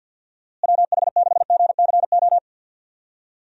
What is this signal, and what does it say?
Morse code, heard as a keyed tone, about 50 wpm, 700 Hz. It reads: WH6GCO
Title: USA Call Signs - 6 Characters - Repeated 3x - 50wpm